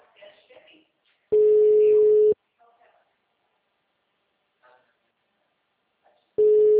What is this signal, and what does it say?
Ringback tone (Europe, 425 Hz), with speech.